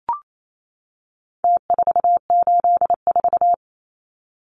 Morse code, heard as a keyed tone, about 28 wpm, 700 Hz. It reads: T484
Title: Numbers - Contest Style - 4-Digit 28wpm